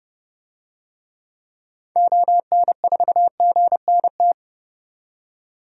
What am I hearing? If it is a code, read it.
Morse code, heard as a keyed tone, about 30 wpm, 700 Hz. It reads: ON4GNT
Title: DXCC-100 Call Signs - 6 Characters - 30wpm